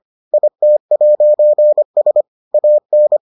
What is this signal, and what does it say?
Morse code, heard as a keyed tone, about 25 wpm, 600 Hz. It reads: IT'S AN